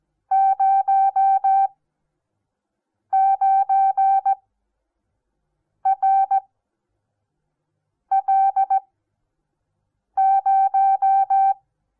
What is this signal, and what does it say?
Morse code, heard as a keyed tone, about 17 wpm, spaced as for 5 wpm, 780 Hz. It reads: 09RL0
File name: Onlinelehrgang / morsen / DJ4UF Morsen08